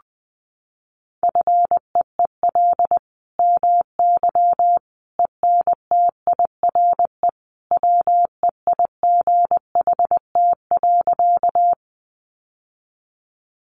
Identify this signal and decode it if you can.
Morse code, heard as a keyed tone, about 20 wpm, 700 Hz. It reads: FEEL MY ENTIRE WEIGHT.
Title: Sentences from Top 800 Words 20wpm